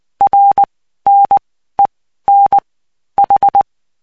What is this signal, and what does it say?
Morse code, heard as a keyed tone, about 20 wpm, spaced as for 13 wpm, 800 Hz. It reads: RNEN H
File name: cw 75BPM